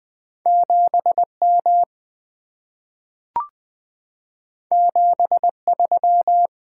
Morse code, heard as a keyed tone, about 20 wpm, 700 Hz. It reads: 7M 73
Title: Two Letter-Number Combinations - 20wpm